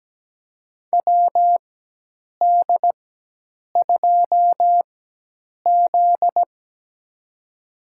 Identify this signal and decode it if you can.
Morse code, heard as a keyed tone, about 17 wpm, spaced as for 8 wpm, 700 Hz. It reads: WD2Z
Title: USA Call Signs - 4x Character Spacing - 17wpm